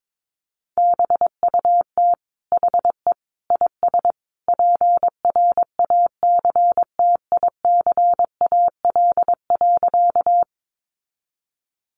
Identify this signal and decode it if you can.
Morse code, heard as a keyed tone, about 22 wpm, 700 Hz. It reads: BUT HE IS PRACTICAL.
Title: Sentences from Top 1400 Words 22wpm